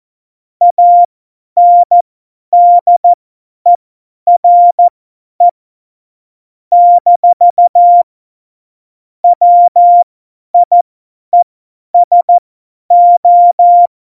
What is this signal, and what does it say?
Morse code, heard as a keyed tone, about 14 wpm, spaced as for 10 wpm, 700 Hz. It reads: ANDERE - WIESO
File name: Morsekurs / Witze / Witz 17 April 2018, 50 F70